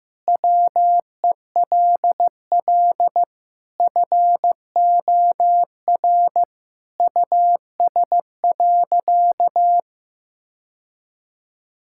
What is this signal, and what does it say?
Morse code, heard as a keyed tone, about 15 wpm, 700 Hz. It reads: WELL FOR US.
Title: Sentences from Top 1000 Words - Review Part 6 - 15wpm